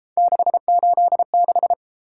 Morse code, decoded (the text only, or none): 686